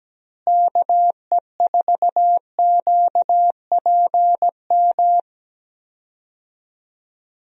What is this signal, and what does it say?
Morse code, heard as a keyed tone, about 17 wpm, 700 Hz. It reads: KE4QPM
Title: USA Call Signs - 6 Characters - Repeated 3x - 17wpm